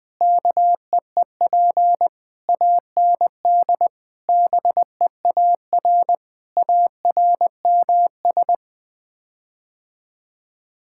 Morse code, heard as a keyed tone, about 21 wpm, 700 Hz. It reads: KEEP AND BEAR ARMS
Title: Sets of 9 Words - 20wpm